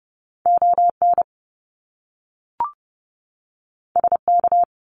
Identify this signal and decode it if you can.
Morse code, heard as a keyed tone, about 30 wpm, 700 Hz. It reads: ON SK
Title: Canadian Provinces and Territories 30wpm